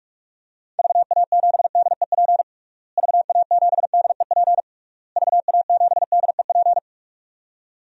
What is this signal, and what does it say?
Morse code, heard as a keyed tone, about 46 wpm, 700 Hz. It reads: VA7DEP VA7DEP VA7DEP
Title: DXCC-100 Call Signs - 6 Characters - Repeated 3x - 45wpm